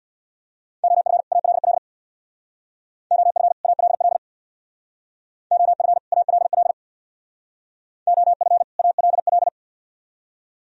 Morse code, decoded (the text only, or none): OF ALL OF ALL OF ALL OF ALL